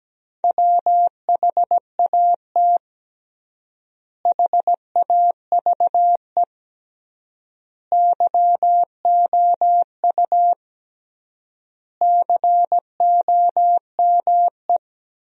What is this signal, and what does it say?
Morse code, heard as a keyed tone, about 17 wpm, 700 Hz. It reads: WHAT HAVE YOU COME
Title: 3x Word-Spacing - Sentences from Top 100 Words - 17wpm